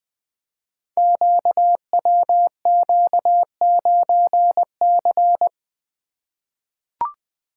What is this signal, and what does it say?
Morse code, heard as a keyed tone, about 20 wpm, 700 Hz. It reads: QWQ9C
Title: Code Groups - 5 Letter:Numbers - 20wpm